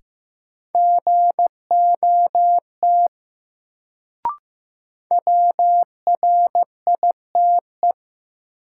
Morse code, heard as a keyed tone, about 15 wpm, 700 Hz. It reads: GOT WRITE
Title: Sentences from Top 300 Words - New Word Drill - 15wpm